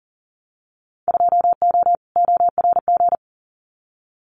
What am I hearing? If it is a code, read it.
Morse code, heard as a keyed tone, about 40 wpm, 700 Hz. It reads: 2O ORG
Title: Winter Field Day Exchanges 40wpm